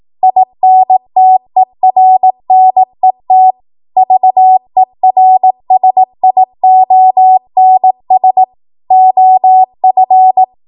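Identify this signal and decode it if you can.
Morse code, heard as a keyed tone, about 18 wpm, 750 Hz. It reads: INTERNET VERSIONS OF
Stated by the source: ARRL 18 wpm Morse code practice archive transmission.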